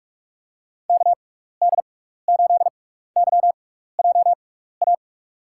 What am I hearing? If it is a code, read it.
Morse code, heard as a keyed tone, about 47 wpm, spaced as for 16 wpm, 700 Hz. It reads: KD8YJA
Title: USA Call Signs - 6 Characters - 6x Character Spacing - 45wpm